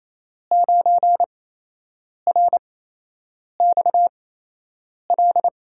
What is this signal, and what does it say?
Morse code, heard as a keyed tone, about 28 wpm, spaced as for 8 wpm, 700 Hz. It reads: 9RXL
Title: USA Call Signs - 6 Characters - 8x Character Spacing - 28wpm